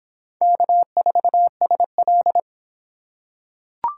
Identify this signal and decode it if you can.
Morse code, heard as a keyed tone, about 25 wpm, 700 Hz. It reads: K4SL